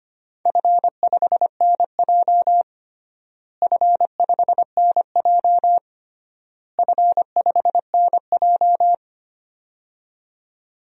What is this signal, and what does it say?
Morse code, heard as a keyed tone, about 25 wpm, 700 Hz. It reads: F5NJ F5NJ F5NJ
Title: DXCC-100 Call Signs - 4 Characters - Repeated 3x - 25wpm